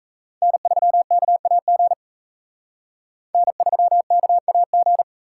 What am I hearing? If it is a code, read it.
Morse code, heard as a keyed tone, about 35 wpm, 700 Hz. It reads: N3KAG N3KAG